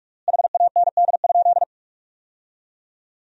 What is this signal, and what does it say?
Morse code, heard as a keyed tone, about 46 wpm, 700 Hz. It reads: HAND?
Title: 3x Word-Spacing - Sentences from Top 200 Words - 45wpm